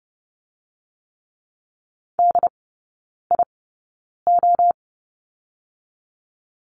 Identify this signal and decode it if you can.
Morse code, heard as a keyed tone, about 30 wpm, spaced as for 9 wpm, 700 Hz. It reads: DIO
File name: 081 Lesson 14e - Number 2 - Three Characters 30wpm